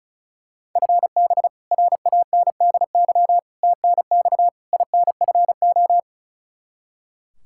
Morse code, heard as a keyed tone, about 36 wpm, 700 Hz. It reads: FB RANDY TNX INFO
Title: K1USN SST QSO Practice 35wpm